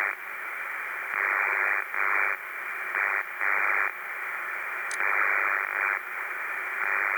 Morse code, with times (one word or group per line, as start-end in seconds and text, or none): none